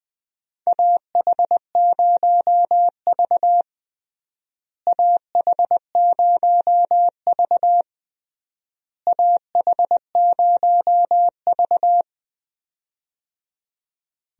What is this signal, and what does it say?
Morse code, heard as a keyed tone, about 20 wpm, 700 Hz. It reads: AH0V AH0V AH0V
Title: USA Call Signs - 4 Characters - Repeated 3x - 20wpm